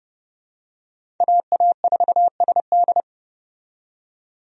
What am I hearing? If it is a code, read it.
Morse code, heard as a keyed tone, about 31 wpm, 700 Hz. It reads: AA4SD
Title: USA Call Signs - 5 Characters - 30wpm